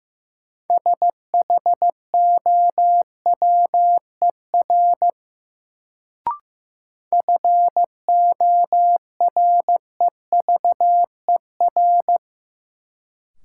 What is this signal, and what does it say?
Morse code, heard as a keyed tone, about 15 wpm, 700 Hz. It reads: SHOWER FOREVER
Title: Sentences from Top 1600 Words - New Word Drill - 15wpm